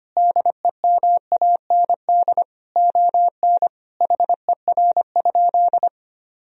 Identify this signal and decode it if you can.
Morse code, heard as a keyed tone, about 25 wpm, 700 Hz. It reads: DEMAND ON HER?
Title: Sentences from Top 1000 Words - Review Part 4 - 25wpm